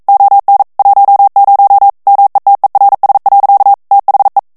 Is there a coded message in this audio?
Yes: Morse code from 0.1 to 4.4 s, heard as a keyed tone, about 42 wpm, 800 Hz. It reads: ON 10 METERS. THE